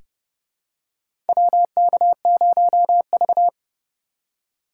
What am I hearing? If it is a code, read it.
Morse code, heard as a keyed tone, about 30 wpm, 700 Hz. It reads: WK0V
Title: USA Call Signs - 4 Characters - 2x Character Spacing - 30wpm